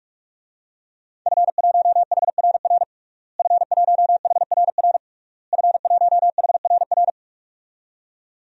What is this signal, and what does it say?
Morse code, heard as a keyed tone, about 46 wpm, 700 Hz. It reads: F1HRR F1HRR F1HRR
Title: DXCC-100 Call Signs - 5 Characters - Repeated 3x - 45wpm